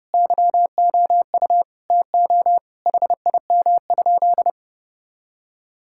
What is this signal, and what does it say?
Morse code, heard as a keyed tone, about 30 wpm, 700 Hz. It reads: YOU TO HIM?
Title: Sentences from Top 1600 Words 30wpm